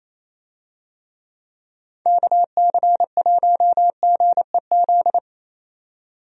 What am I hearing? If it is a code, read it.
Morse code, heard as a keyed tone, about 28 wpm, 700 Hz. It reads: KC1GEZ
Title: USA Call Signs - 6 Characters - 28wpm